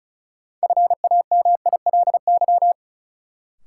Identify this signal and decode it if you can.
Morse code, heard as a keyed tone, about 34 wpm, 700 Hz. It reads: FAMILY
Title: Sentences from Top 400 Words - New Word Drill - 35wpm